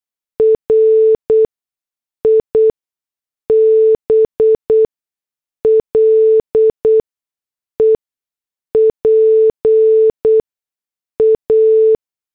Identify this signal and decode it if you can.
Morse code, heard as a keyed tone, about 8 wpm, spaced as for 6 wpm, 440 Hz. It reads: RIBLE PA